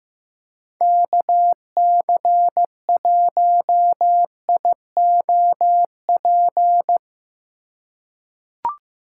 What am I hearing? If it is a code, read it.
Morse code, heard as a keyed tone, about 16 wpm, 700 Hz. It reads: KC1IOP
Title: USA Call Signs - 6 Characters - 8x Character Spacing - 15wpm